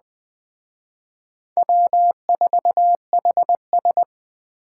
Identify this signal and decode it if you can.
Morse code, heard as a keyed tone, about 20 wpm, 700 Hz. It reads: W4HS
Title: Speed Racing - USA Call Signs 17wpm